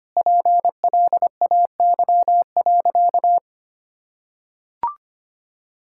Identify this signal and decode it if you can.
Morse code, heard as a keyed tone, about 25 wpm, 700 Hz. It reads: PLAY.
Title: Sentences from Top 1000 Words - Review Part 6 - 25wpm